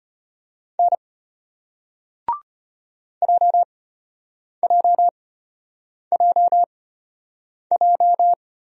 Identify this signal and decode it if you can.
Morse code, heard as a keyed tone, about 34 wpm, spaced as for 8 wpm, 700 Hz. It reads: N JJJJ